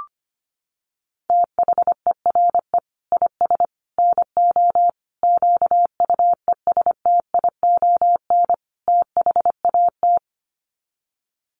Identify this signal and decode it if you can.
Morse code, heard as a keyed tone, about 25 wpm, 700 Hz. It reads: THERE IS NO QUESTION THAT
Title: Sets of 5 Words Encore 25wpm